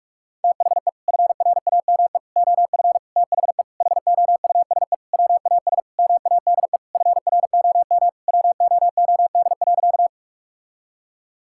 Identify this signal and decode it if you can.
Morse code, heard as a keyed tone, about 41 wpm, 700 Hz. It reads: THE FRAME OF THE HOUSE WAS MADE FROM WOOD.
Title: Sentences from Top 1000 Words - Review Part 6 - 45wpm